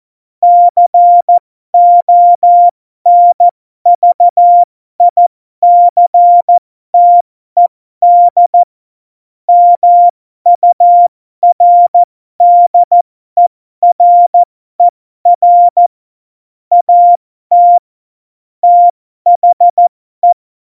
Morse code, heard as a keyed tone, about 14 wpm, spaced as for 12 wpm, 700 Hz. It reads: CONVICTED MURDERER AT THE
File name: Morsekurs / Witze